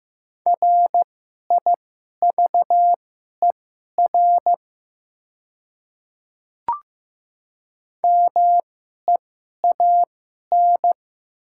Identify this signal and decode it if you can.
Morse code, heard as a keyed tone, about 16 wpm, spaced as for 11 wpm, 700 Hz. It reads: RIVER MEAN
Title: 2x Character Spacing - Top 500 Words - 15wpm